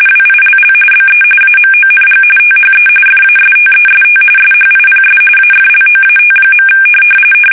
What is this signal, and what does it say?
RTTY, about 45.45 baud. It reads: WILL BE AMONG THE FASTEST SHIPS AFLOAT.
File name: fsk850 / battleshipnj